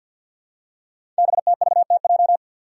Morse code, heard as a keyed tone, about 48 wpm, 700 Hz. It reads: BTVTJ
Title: Code Groups - 5 Letters - 50wpm